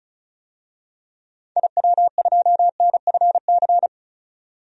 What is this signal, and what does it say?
Morse code, heard as a keyed tone, about 36 wpm, 700 Hz. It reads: IW2NFC
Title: DXCC-100 Call Signs - 6 Characters - 35wpm